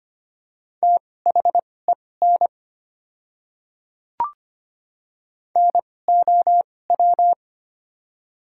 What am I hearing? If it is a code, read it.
Morse code, heard as a keyed tone, about 25 wpm, spaced as for 18 wpm, 700 Hz. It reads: THEN NOW
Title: 2x Character Spacing - Top 100 Words - 25wpm